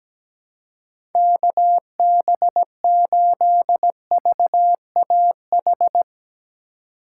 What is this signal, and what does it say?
Morse code, heard as a keyed tone, about 17 wpm, 700 Hz. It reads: KB8VAH
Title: USA Call Signs - 6 Characters - 4x Character Spacing - 17wpm